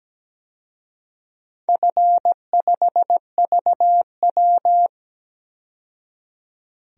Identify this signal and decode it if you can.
Morse code, heard as a keyed tone, about 17 wpm, 700 Hz. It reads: F5VW
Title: DXCC-100 Call Signs - 4 Characters - 17wpm